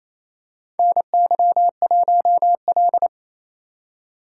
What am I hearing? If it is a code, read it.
Morse code, heard as a keyed tone, about 28 wpm, 700 Hz. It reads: NY1L
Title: USA Call Signs - 4 Characters - 3x Character Spacing - 28wpm